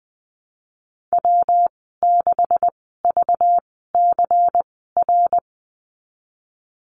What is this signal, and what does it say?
Morse code, heard as a keyed tone, about 20 wpm, spaced as for 15 wpm, 700 Hz. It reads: W6VCR
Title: 193 Lesson 31g - Letter v - Call Signs 20wpm